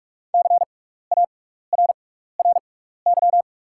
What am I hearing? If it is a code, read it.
Morse code, heard as a keyed tone, about 44 wpm, spaced as for 15 wpm, 700 Hz. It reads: CARRY
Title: 6x Character Spacing - Top 300 Words - 45wpm